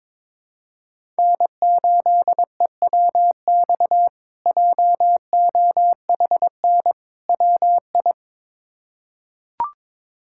Morse code, heard as a keyed tone, about 21 wpm, 700 Hz. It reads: N8EWX JOHN WI
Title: K1USN SST QSO Practice - 22wpm